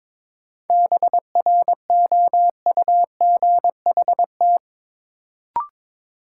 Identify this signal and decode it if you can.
Morse code, heard as a keyed tone, about 22 wpm, 700 Hz. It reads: BROUGHT
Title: Sentences from Top 500 Words - New Word Drill - 22wpm